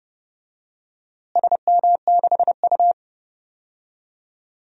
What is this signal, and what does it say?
Morse code, heard as a keyed tone, about 32 wpm, 700 Hz. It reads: SM6U
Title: DXCC-100 Call Signs - 4 Characters - Repeated 2x - 30wpm